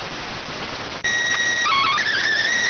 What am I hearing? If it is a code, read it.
SSTV (slow-scan television), PD 290.